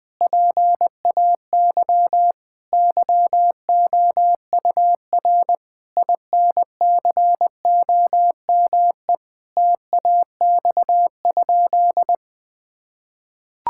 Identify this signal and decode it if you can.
Morse code, heard as a keyed tone, about 21 wpm, 700 Hz. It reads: PAY YOUR INCOME TAX?
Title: Sentences from Top 1000 Words - Review Part 5 - 20wpm